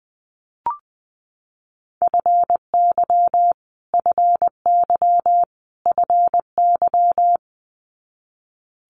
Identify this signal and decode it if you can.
Morse code, heard as a keyed tone, about 20 wpm, 700 Hz. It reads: FY FY FY
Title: Multi-letter Phonemes Repeated 3x 20wpm